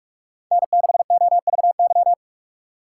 Morse code, heard as a keyed tone, about 45 wpm, 700 Hz. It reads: N6OVY
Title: USA Call Signs - 5 Characters - 6x Character Spacing - 45wpm